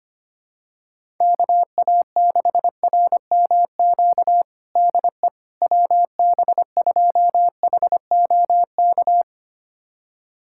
Morse code, heard as a keyed tone, about 25 wpm, 700 Hz. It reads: KA6RMQ DE WB2HOK
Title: USA Callsign DE Callsign - 6 Characters - 25wpm